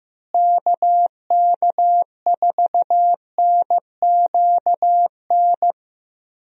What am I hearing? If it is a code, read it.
Morse code, heard as a keyed tone, about 15 wpm, 700 Hz. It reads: KK4NQN
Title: USA Call Signs - 6 Characters - Repeated 3x - 15wpm